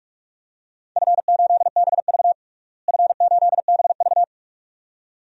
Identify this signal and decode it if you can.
Morse code, heard as a keyed tone, about 43 wpm, 700 Hz. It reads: F8BV F8BV
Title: DXCC-100 Call Signs - 4 Characters - Repeated 2x - 45wpm